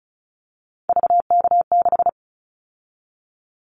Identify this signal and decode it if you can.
Morse code, heard as a keyed tone, about 34 wpm, 700 Hz. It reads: VK6
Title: Callsign Prefixes for Top 100 DXCC - 35wpm